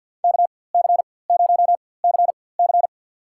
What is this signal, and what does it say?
Morse code, heard as a keyed tone, about 51 wpm, spaced as for 23 wpm, 700 Hz. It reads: KC0CC
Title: USA Call Signs - 5 Characters - 4x Character Spacing - 50wpm